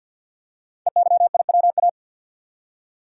Morse code, heard as a keyed tone, about 46 wpm, 700 Hz. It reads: EYIWA
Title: Code Groups - 5 Letters - 50wpm